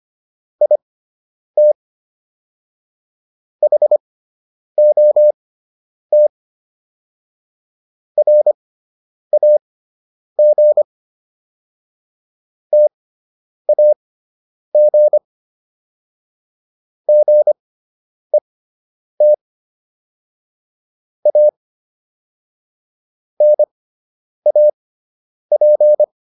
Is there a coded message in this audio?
Yes: Morse code from 0.6 to 26.0 s, heard as a keyed tone, about 25 wpm, spaced as for 9 wpm, 600 Hz. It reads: IT HOT RAG TAG GET A NAP